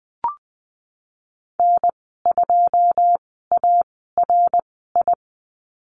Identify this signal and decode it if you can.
Morse code, heard as a keyed tone, about 20 wpm, spaced as for 15 wpm, 700 Hz. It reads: N2ARI
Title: 082 Lesson 14g - Number 2 - Call Signs 20wpm